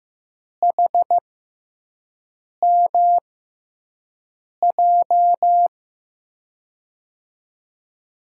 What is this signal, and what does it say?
Morse code, heard as a keyed tone, about 15 wpm, spaced as for 5 wpm, 700 Hz. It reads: HMJ
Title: USA Call Signs - 6 Characters - 6x Character Spacing - 15wpm